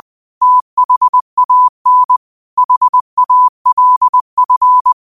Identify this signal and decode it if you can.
Morse code, heard as a keyed tone, about 20 wpm, 1000 Hz. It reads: THAN HALF